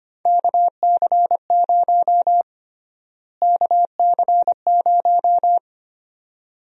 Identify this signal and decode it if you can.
Morse code, heard as a keyed tone, about 25 wpm, 700 Hz. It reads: KC0 KC0
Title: USA Call Sign Prefixes - Repeated 2x - 25wpm